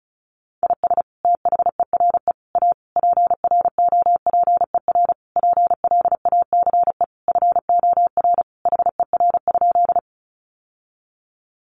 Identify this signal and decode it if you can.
Morse code, heard as a keyed tone, about 34 wpm, 700 Hz. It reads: IS THERE A PROPER PLACE FOR HER?
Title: Sentences from Top 800 Words 35wpm